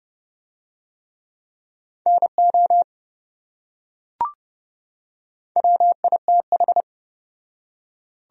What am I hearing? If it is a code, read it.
Morse code, heard as a keyed tone, about 28 wpm, 700 Hz. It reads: NO WITH